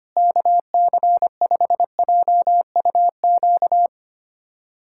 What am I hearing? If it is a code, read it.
Morse code, heard as a keyed tone, about 25 wpm, 700 Hz. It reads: KC5JUQ
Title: USA Callsign DE Callsign - 6 Characters - 25wpm